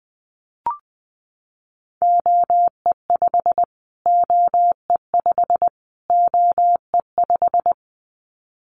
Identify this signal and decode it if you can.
Morse code, heard as a keyed tone, about 20 wpm, 700 Hz. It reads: OE5 OE5 OE5
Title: Callsign Prefixes for Top 25 DXCC - Repeated 3x - 20wpm